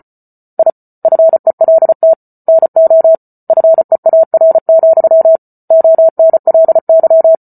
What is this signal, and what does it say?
Morse code, heard as a keyed tone, about 34 wpm, 650 Hz. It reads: I FELT NO FEAR, ONLY
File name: Book Two / 08-Chapter 8 - 34 WPM